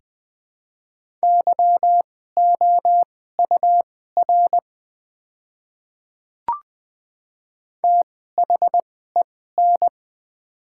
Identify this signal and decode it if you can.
Morse code, heard as a keyed tone, about 20 wpm, spaced as for 14 wpm, 700 Hz. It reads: YOUR THEN
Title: 2x Character Spacing - Top 100 Words - 20wpm